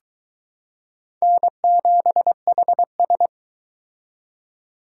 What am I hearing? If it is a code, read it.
Morse code, heard as a keyed tone, about 22 wpm, 700 Hz. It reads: N7HS